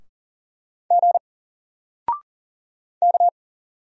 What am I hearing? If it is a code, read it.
Morse code, heard as a keyed tone, about 40 wpm, 700 Hz. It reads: G K
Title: Single Letter-Number 40wpm